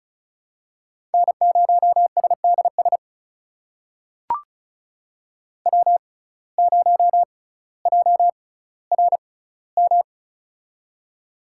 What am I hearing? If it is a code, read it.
Morse code, heard as a keyed tone, about 34 wpm, spaced as for 18 wpm, 700 Hz. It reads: N0SDS W0JRM
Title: USA Call Signs - 5 Characters - 6x Character Spacing - 35wpm